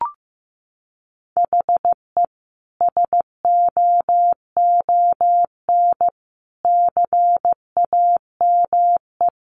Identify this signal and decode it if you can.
Morse code, heard as a keyed tone, about 15 wpm, 700 Hz. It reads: HE SOON CAME
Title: Sentences from Top 600 Words 15wpm